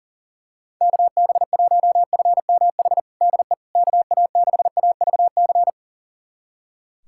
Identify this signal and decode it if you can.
Morse code, heard as a keyed tone, about 40 wpm, 700 Hz. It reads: KB1FMH DE KA6AVC
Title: USA Callsign DE Callsign - 6 Characters - 40wpm